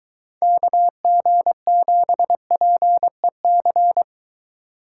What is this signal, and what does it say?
Morse code, heard as a keyed tone, about 23 wpm, 700 Hz. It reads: KG7PEC